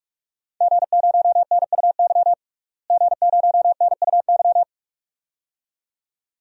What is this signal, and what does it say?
Morse code, heard as a keyed tone, about 46 wpm, 700 Hz. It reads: G0NUY G0NUY
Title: DXCC-100 Call Signs - 5 Characters - Repeated 2x - 45wpm